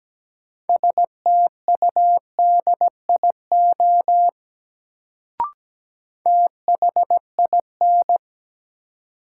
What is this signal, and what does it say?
Morse code, heard as a keyed tone, about 17 wpm, 700 Hz. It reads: STUDIO THIN
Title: Sentences from Top 1200 Words - New Word Drill Alt - 17wpm